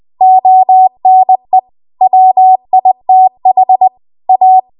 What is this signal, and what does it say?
Morse code, heard as a keyed tone, about 20 wpm, 750 Hz. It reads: ONE WITH A
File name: Archive / 20 WPM / 20WPM